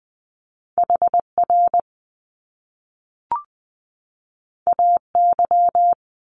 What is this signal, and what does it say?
Morse code, heard as a keyed tone, about 20 wpm, 700 Hz. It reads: HR AY
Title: Top Two Letter Combinations in Words 20wpm